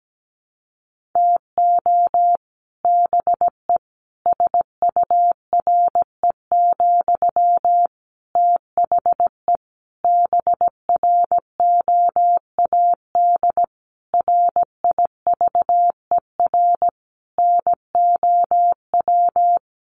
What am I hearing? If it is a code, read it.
Morse code, heard as a keyed tone, about 17 wpm, 700 Hz. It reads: TO BE SURE, THE BROAD RIVER NOW
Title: Sentences from Top 1000 Words 17wpm